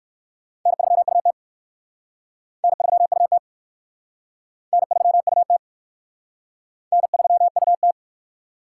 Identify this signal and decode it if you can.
Morse code, heard as a keyed tone, about 54 wpm, 700 Hz. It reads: N3UT N3UT N3UT N3UT
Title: Speed Racing - USA 1x2 Call Signs 45wpm